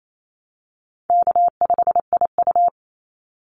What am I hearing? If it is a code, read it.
Morse code, heard as a keyed tone, about 28 wpm, 700 Hz. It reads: K5IU
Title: Easy Call Signs Repeated 2x - 28wpm